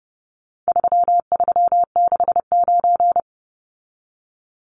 Morse code, heard as a keyed tone, about 30 wpm, 700 Hz. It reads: 3369
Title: Numbers - 4-Digit 30wpm